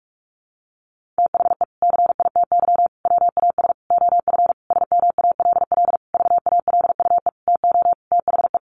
Morse code, heard as a keyed tone, about 44 wpm, 700 Hz. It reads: THE CITY WAS OF SMALL VALUE TO THE
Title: Sentences from Top 1500 Words 45wpm